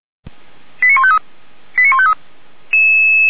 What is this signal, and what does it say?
Five-tone selective call (ZVEI), heard as quick runs of five tones, tones of about 70 ms. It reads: 89153 89153